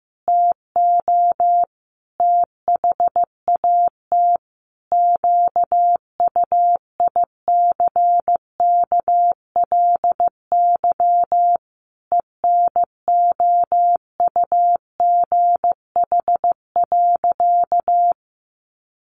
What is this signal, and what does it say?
Morse code, heard as a keyed tone, about 15 wpm, 700 Hz. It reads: TO THAT QUICKLY ENOUGH.
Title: Sentences from Top 700 Words 15wpm